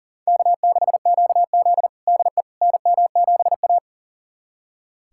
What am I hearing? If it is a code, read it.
Morse code, heard as a keyed tone, about 40 wpm, 700 Hz. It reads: K6QZ DE NM7A
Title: USA Callsign DE Callsign - 4 Characters - 40wpm